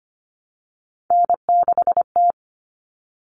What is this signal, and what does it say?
Morse code, heard as a keyed tone, about 25 wpm, 700 Hz. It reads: N6T
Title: All Cut Numbers - 3-Digit 25wpm